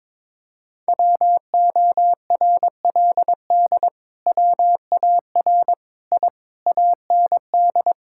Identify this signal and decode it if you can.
Morse code, heard as a keyed tone, about 22 wpm, 700 Hz. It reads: WORLD WAR I AND
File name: practice / 22wpm / Sets of 7 Words - 22wpm